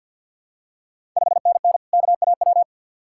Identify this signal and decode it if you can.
Morse code, heard as a keyed tone, about 45 wpm, 700 Hz. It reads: 5NN KAW
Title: Ontario QSO Party Exchange 50wpm